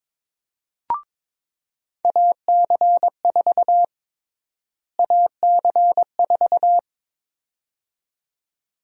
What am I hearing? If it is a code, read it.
Morse code, heard as a keyed tone, about 22 wpm, 700 Hz. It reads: AC4 AC4
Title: USA Call Sign Prefixes - Repeated 2x - 22wpm